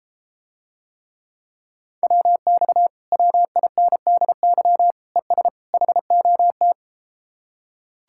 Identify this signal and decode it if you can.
Morse code, heard as a keyed tone, about 33 wpm, 700 Hz. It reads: WX WINDY ES HOT